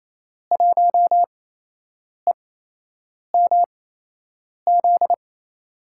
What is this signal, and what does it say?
Morse code, heard as a keyed tone, about 27 wpm, spaced as for 8 wpm, 700 Hz. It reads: 1EMZ
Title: USA Call Signs - 6 Characters - 8x Character Spacing - 28wpm